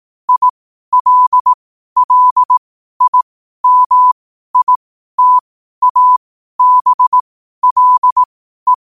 Morse code, heard as a keyed tone, about 18 wpm, spaced as for 12 wpm, 1000 Hz. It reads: ILLIMITABLE
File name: 12wpm